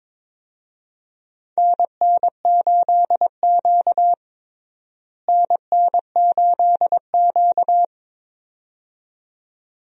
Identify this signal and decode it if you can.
Morse code, heard as a keyed tone, about 22 wpm, 700 Hz. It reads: NN8Q NN8Q
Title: USA Call Signs - 4 Characters - Repeated 2x - 22wpm